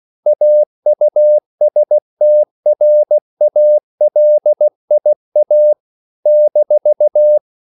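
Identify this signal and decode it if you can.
Morse code, heard as a keyed tone, about 16 wpm, 600 Hz. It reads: AUSTRALIA -
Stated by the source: CW practice of national capitals at 16 words per minute, from ON6ZQ.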